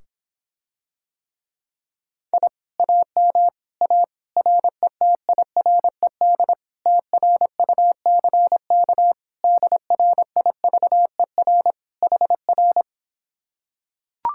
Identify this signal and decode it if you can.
Morse code, heard as a keyed tone, about 26 wpm, 700 Hz. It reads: I AM A RETIRED TRUCK DRIVER HR